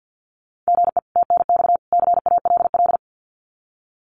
Morse code, heard as a keyed tone, about 50 wpm, 700 Hz. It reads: GE TNX CALL
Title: Common QSO Phrases 50wpm